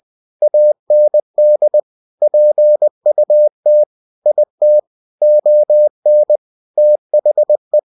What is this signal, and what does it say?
Morse code, heard as a keyed tone, about 20 wpm, 600 Hz. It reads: AND PUT IT ON THE
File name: cw-books / winnie-the-pooh / winnie-20wpm-page-0058